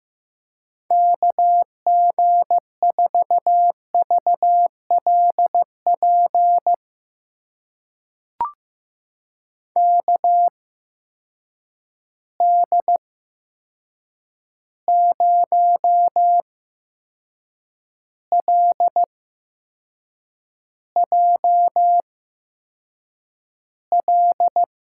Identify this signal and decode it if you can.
Morse code, heard as a keyed tone, about 15 wpm, spaced as for 6 wpm, 700 Hz. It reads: KG4VLP KD0LJL
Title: USA Call Signs - 6 Characters - 8x Character Spacing - 15wpm